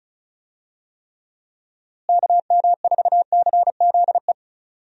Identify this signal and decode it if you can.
Morse code, heard as a keyed tone, about 34 wpm, 700 Hz. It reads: KM4CZE
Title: USA Call Signs - 6 Characters - Repeated 3x - 35wpm